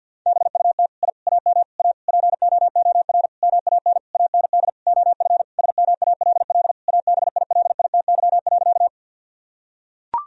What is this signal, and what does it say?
Morse code, heard as a keyed tone, about 46 wpm, 700 Hz. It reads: BUT I AM A POOR MAN AND OF SMALL ABILITY.
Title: Sentences from Top 1000 Words - Review Part 3 - 50wpm